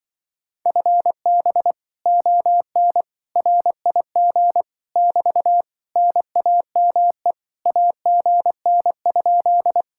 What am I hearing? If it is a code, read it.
Morse code, heard as a keyed tone, about 24 wpm, 700 Hz. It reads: FB ON RIG <BT> NAME AGN?
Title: Speed-Racing Common QSO Phrases - 20wpm